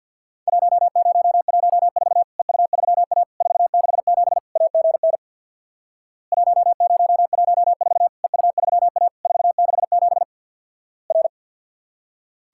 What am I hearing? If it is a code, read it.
Morse code, heard as a keyed tone, about 53 wpm, 700 Hz. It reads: 1014 EU3A 467 AGN 1014 EU3A 467 R
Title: Worked All Europe DX Contest - QTC Exchange with Repeats - 50wpm